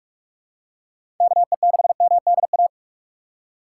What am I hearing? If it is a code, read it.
Morse code, heard as a keyed tone, about 46 wpm, 700 Hz. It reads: KE6MDA
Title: USA Call Signs - 3x Character Spacing - 45wpm